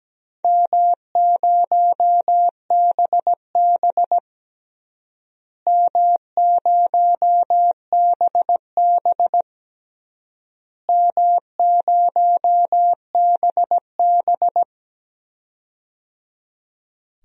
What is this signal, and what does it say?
Morse code, heard as a keyed tone, about 17 wpm, 700 Hz. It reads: M0BB M0BB M0BB
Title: DXCC-100 Call Signs - 4 Characters - Repeated 3x - 17wpm